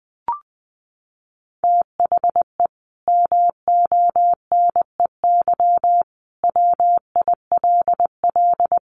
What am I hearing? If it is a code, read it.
Morse code, heard as a keyed tone, about 20 wpm, 700 Hz. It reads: THE MONEY WILL
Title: Sentences from Top 1500 Words 20wpm